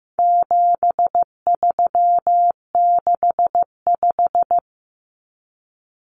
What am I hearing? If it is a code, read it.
Morse code, heard as a keyed tone, about 15 wpm, 700 Hz. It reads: 7365
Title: Common Cut Numbers - 4-Digit 15wpm